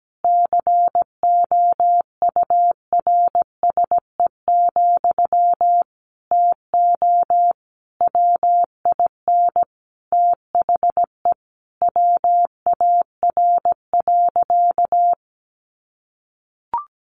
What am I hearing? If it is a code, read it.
Morse code, heard as a keyed tone, about 17 wpm, 700 Hz. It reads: COURSE, TO WIN THE WAR.
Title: Sentences from Top 1500 Words 17wpm